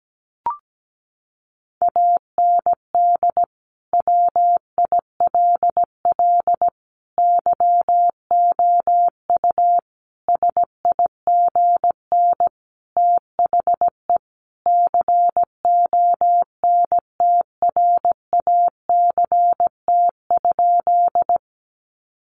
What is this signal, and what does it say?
Morse code, heard as a keyed tone, about 17 wpm, 700 Hz. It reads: AND WILL YOU SIGN THE CONTRACT?
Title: Sentences from Top 1400 Words 17wpm